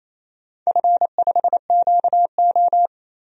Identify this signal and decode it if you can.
Morse code, heard as a keyed tone, about 28 wpm, 700 Hz. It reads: F5QO
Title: DXCC-100 Call Signs - 4 Characters - 28wpm